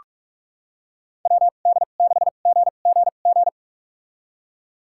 Morse code, heard as a keyed tone, about 46 wpm, spaced as for 32 wpm, 700 Hz. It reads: WD6GGG
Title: USA Call Signs - 6 Characters - 2x Character Spacing - 45wpm